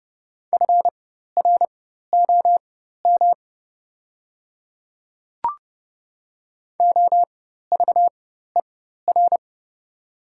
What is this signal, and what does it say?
Morse code, heard as a keyed tone, about 30 wpm, spaced as for 14 wpm, 700 Hz. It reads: FROM OVER
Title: 4x Character Spacing - Top 100 Words - 30wpm